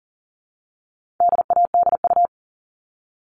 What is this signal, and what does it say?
Morse code, heard as a keyed tone, about 40 wpm, 700 Hz. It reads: DADU
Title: All Cut Numbers - 4-Digit 40wpm